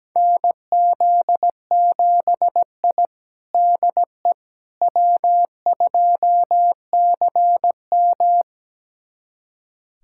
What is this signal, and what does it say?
Morse code, heard as a keyed tone, about 17 wpm, 700 Hz. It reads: NZ7I DE W2CM
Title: USA Callsign DE Callsign - 4 Characters - 17wpm